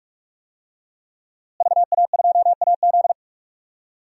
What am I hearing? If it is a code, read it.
Morse code, heard as a keyed tone, about 47 wpm, 700 Hz. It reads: VA2AZ
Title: DXCC-100 Call Signs - 5 Characters - Repeated 3x - 45wpm